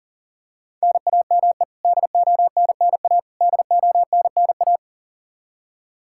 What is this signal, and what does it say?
Morse code, heard as a keyed tone, about 40 wpm, 700 Hz. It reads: NAME DONNA DONNA
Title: K1USN SST QSO Practice 40wpm